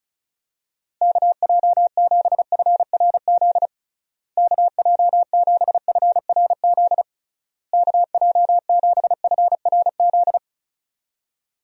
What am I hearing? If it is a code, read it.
Morse code, heard as a keyed tone, about 36 wpm, 700 Hz. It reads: KJ7FRZ KJ7FRZ KJ7FRZ
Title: USA Call Signs - 6 Characters - Repeated 3x - 35wpm